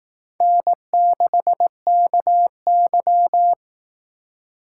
Morse code, heard as a keyed tone, about 18 wpm, 700 Hz. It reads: N6KY
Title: Speed Racing - USA Call Signs 15wpm